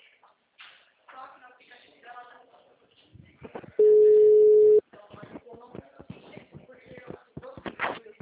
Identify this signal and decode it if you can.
Ringback tone (Europe, 425 Hz), with speech.
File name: out- -2007- - - .418